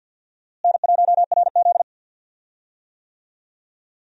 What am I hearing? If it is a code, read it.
Morse code, heard as a keyed tone, about 52 wpm, 700 Hz. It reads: N1RZ
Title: USA Call Signs - 4 Characters - Repeated 3x - 50wpm